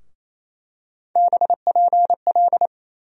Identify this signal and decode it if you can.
Morse code, heard as a keyed tone, about 28 wpm, 700 Hz. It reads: BPL
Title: USA Call Sign Postfixes - 3 Letters 28wpm